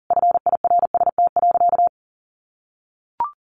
Morse code, heard as a keyed tone, about 40 wpm, 700 Hz. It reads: FIRST.
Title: Sentences from Top 300 Words 40wpm